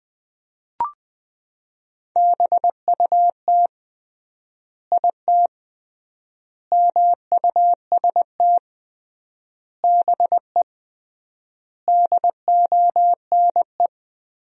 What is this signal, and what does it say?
Morse code, heard as a keyed tone, about 21 wpm, 700 Hz. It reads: BUT IT MUST BE DONE
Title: 3x Word-Spacing - Sentences from Top 500 Words - 20wpm